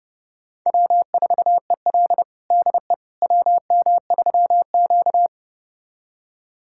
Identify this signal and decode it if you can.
Morse code, heard as a keyed tone, about 30 wpm, 700 Hz. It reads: W4EL DE WM3Q
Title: USA Callsign DE Callsign - 4 Characters - 30wpm